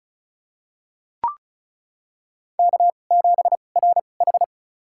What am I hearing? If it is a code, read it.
Morse code, heard as a keyed tone, about 34 wpm, spaced as for 25 wpm, 690 Hz. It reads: K7RH
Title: USA Call Signs - 4 Characters - 2x Character Spacing - 35wpm